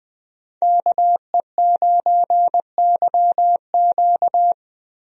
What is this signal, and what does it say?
Morse code, heard as a keyed tone, about 20 wpm, 700 Hz. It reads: KE9YQ
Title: USA Call Signs - 5 Characters - Repeated 2x - 20wpm